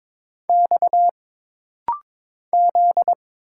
Morse code, heard as a keyed tone, about 22 wpm, 700 Hz. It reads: X Z
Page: This rapid-fire timing is geared to push your ICR (Instant Character Recognition) skills.